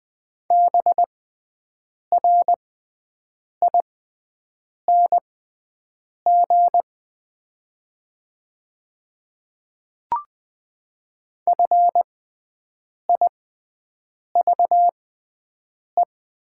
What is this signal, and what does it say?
Morse code, heard as a keyed tone, about 20 wpm, spaced as for 7 wpm, 700 Hz. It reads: BRING FIVE
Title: 6x Character Spacing - Top 500 Words - 20wpm